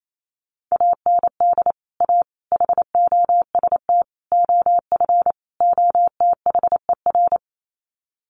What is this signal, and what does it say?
Morse code, heard as a keyed tone, about 28 wpm, 700 Hz. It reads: AND A HOST OF OTHER
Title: Sets of 5 Words Encore2 28wpm